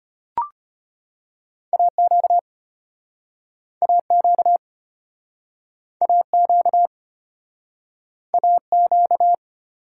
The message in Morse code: AQ AQ AQ AQ

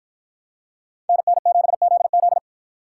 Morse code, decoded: NN7ZZ

